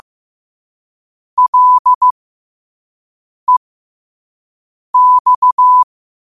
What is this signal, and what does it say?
Morse code, heard as a keyed tone, about 15 wpm, spaced as for 5 wpm, 1000 Hz. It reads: LEX
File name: qotdfiles / 5wpm